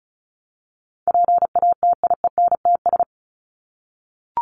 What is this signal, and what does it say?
Morse code, heard as a keyed tone, about 34 wpm, 700 Hz. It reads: PATIENTS
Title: ES:S Suffix - Top 100 Words with Base Words 35wpm